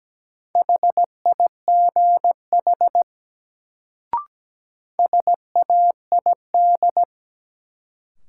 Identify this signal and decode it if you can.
Morse code, heard as a keyed tone, about 17 wpm, 700 Hz. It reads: HIGH SAID
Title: Sets of 2 Words - New Word Drill - 17wpm